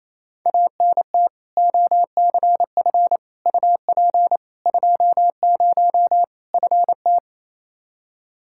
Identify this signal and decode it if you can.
Morse code, heard as a keyed tone, about 28 wpm, 700 Hz. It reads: ANT OCF UP 20 FT
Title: K1USN SST QSO Practice 28wpm